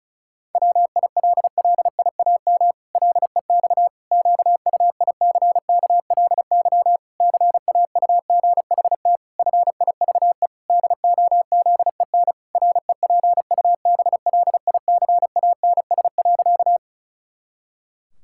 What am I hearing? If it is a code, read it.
Morse code, heard as a keyed tone, about 36 wpm, 700 Hz. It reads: WILLIAM LEX QUICKLY CAUGHT FIVE DOZEN REPUBLICANS.
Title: Pangram Challenge v2 - 35wpm